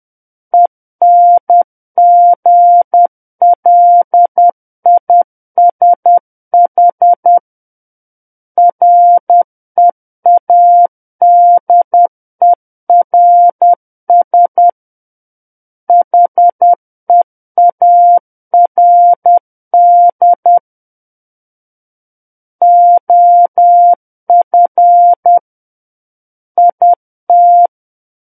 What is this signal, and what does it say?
Morse code, heard as a keyed tone, about 10 wpm, 700 Hz. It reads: ENGLISH READERS HEARD OF IT
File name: Book One / 01-Chapter 1 - 10 WPM